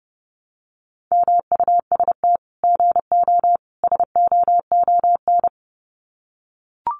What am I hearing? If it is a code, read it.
Morse code, heard as a keyed tone, about 30 wpm, 700 Hz. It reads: MUST GO SOON
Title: Common QSO Phrases 30wpm